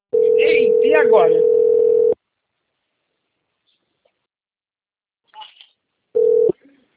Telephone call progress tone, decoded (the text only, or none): ringback tone